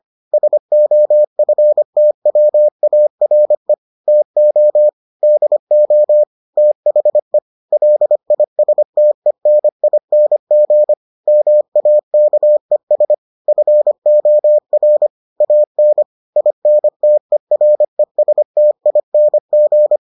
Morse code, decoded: SOFTWARE TO DO THE LISTENING MAKES FOR AN INTERESTING

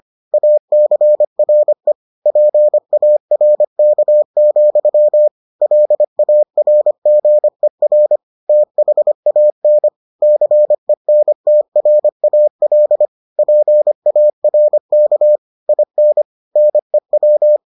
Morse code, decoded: ACRE PARK, LARGER THAN CENTRAL PARK IN NEW

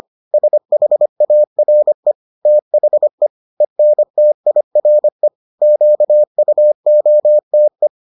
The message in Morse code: SHARE THE ENTIRE QUOTE

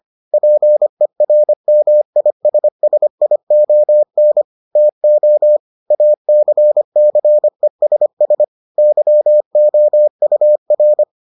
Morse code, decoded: PERMISSION TO ACCESS YOUR

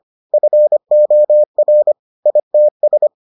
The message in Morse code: FOR ITS

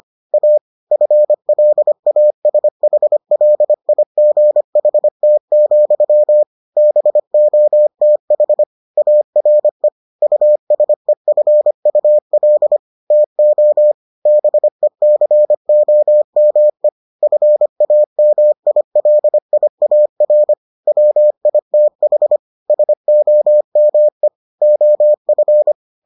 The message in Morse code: A FLASHLIGHT, BOTH ARE USEFUL TO BECOME FAMILIAR WITH SOME OF

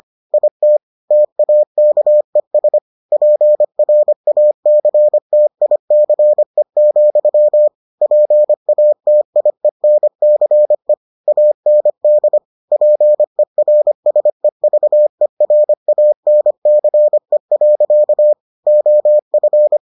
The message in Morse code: IT TAKES PRACTICE, PATIENCE AND PERSEVERANCE. OF

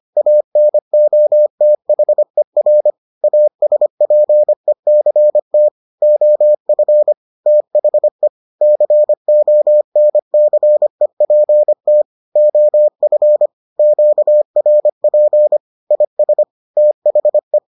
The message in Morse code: ANOTHER ASPECT OF THE CONCEPT OF QRP IS THE

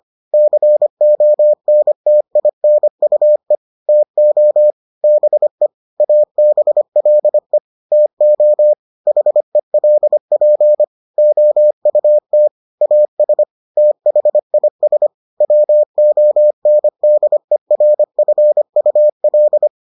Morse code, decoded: CONTINUE TO BE ABLE TO HELP OUT AS THIS WONDERFUL

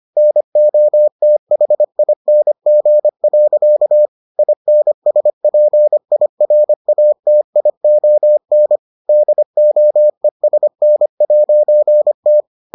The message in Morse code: NOTHING. INSPIRATION DOESN'T

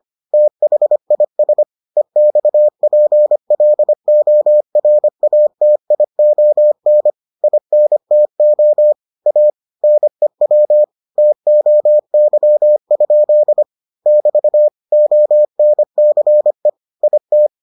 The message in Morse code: THIS EXPLORATION INTO A NEW TOY? <BT> ONCE IT

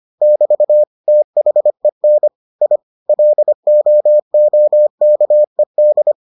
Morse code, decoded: <BT> THEN I LOOKED

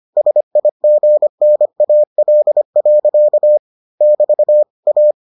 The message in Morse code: SIGNAL. <BT> A

